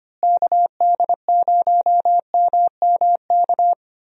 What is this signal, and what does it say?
Morse code, heard as a keyed tone, about 25 wpm, 700 Hz. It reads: KD0MMK
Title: USA Call Signs - 6 Characters - Repeated 3x - 25wpm